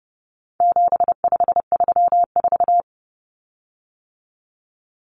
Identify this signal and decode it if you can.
Morse code, heard as a keyed tone, about 30 wpm, 700 Hz. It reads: 7534
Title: Common Cut Numbers - 4-Digit 30wpm